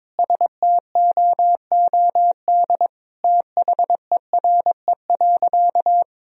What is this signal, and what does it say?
Morse code, heard as a keyed tone, about 21 wpm, 700 Hz. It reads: STOOD THERE.
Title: Sentences from Top 1000 Words - Review Part 2 - 22wpm